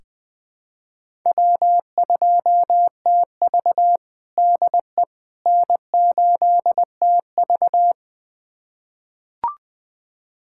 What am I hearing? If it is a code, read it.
Morse code, heard as a keyed tone, about 20 wpm, 700 Hz. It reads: W2TV DE N8TV
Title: USA Callsign DE Callsign - 4 Characters - 20wpm